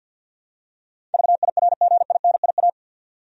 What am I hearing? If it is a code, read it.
Morse code, heard as a keyed tone, about 52 wpm, 700 Hz. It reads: VIRGINIA
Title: USA Common Names - 50wpm